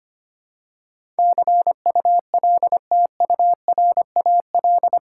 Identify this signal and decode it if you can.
Morse code, heard as a keyed tone, about 25 wpm, 700 Hz. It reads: CULTURAL
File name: 8 Letter Words - First 3 Letters Given - 25wpm